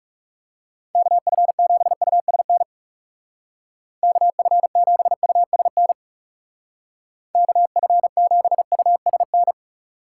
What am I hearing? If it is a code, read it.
Morse code, heard as a keyed tone, about 37 wpm, 700 Hz. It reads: KF7USN KF7USN KF7USN
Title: Speed Racing - USA 2x3 Call Signs 30wpm